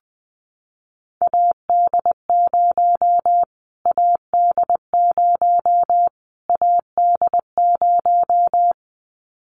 Morse code, heard as a keyed tone, about 20 wpm, 700 Hz. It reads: AD0 AD0 AD0
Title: Callsign Prefixes for Top 25 DXCC - Repeated 3x - 20wpm